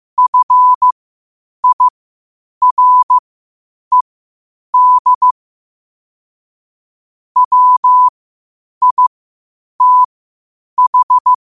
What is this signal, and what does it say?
Morse code, heard as a keyed tone, about 15 wpm, spaced as for 8 wpm, 1000 Hz. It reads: FIRED WITH
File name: qotdfiles / 8wpm